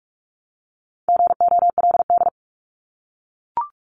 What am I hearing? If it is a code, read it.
Morse code, heard as a keyed tone, about 44 wpm, 700 Hz. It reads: GOLD
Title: Top 500 Words Repeated 45wpm